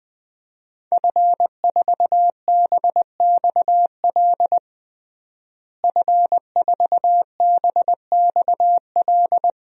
Morse code, heard as a keyed tone, about 20 wpm, 700 Hz. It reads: F4BXL F4BXL
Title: DXCC-100 Call Signs - 5 Characters - Repeated 3x - 20wpm